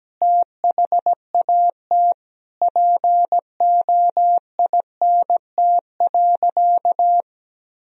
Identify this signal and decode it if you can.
Morse code, heard as a keyed tone, about 17 wpm, 700 Hz. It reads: THAT POINT.
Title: Sentences from Top 1000 Words - Review Part 1 - 17wpm